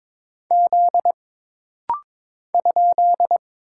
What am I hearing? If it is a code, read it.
Morse code, heard as a keyed tone, about 22 wpm, 700 Hz. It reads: Z ?
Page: This rapid-fire timing is geared to push your ICR (Instant Character Recognition) skills.